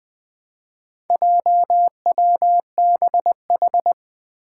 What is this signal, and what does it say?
Morse code, heard as a keyed tone, about 20 wpm, 700 Hz. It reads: JWBH
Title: Four Challenging Letter-Number Combinations 20wpm